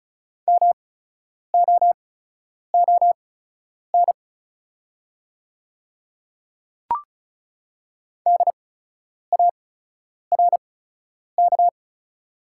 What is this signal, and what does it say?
Morse code, heard as a keyed tone, about 36 wpm, spaced as for 10 wpm, 690 Hz. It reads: MOON DARK
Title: 8x Character Spacing - Top 500 Words - 35wpm